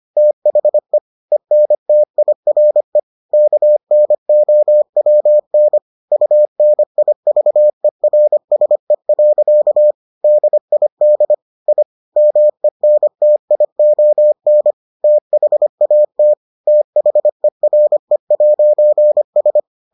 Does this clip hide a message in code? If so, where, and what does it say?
Yes: Morse code from 0.2 to 19.6 s, heard as a keyed tone, about 25 wpm, 600 Hz. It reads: THE ENTIRE KNOWN UNIVERSE. DID I MENTION THAT THERE'S